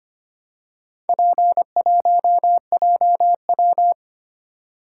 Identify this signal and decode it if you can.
Morse code, heard as a keyed tone, about 25 wpm, 700 Hz. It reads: P1JW
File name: practice / 25wpm / Four Letter-Number Combinations - 25wpm